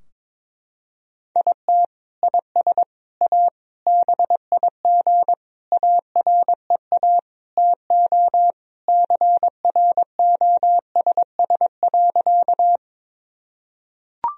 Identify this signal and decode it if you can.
Morse code, heard as a keyed tone, about 21 wpm, 700 Hz. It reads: IT IS A BIG AREA TO CROSS.
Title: Sentences from Top 1000 Words - Review Part 2 - 22wpm